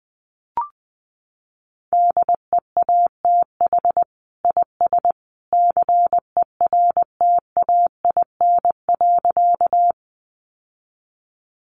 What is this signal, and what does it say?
Morse code, heard as a keyed tone, about 20 wpm, 700 Hz. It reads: DEATH IS CERTAIN.
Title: Sentences from Top 1100 Words 20wpm